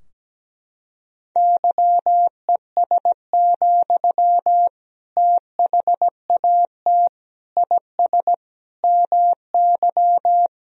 Morse code, heard as a keyed tone, about 17 wpm, 700 Hz. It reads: YES, THAT IS MY
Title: Sentences from Top 1000 Words - Review Part 3 - 17wpm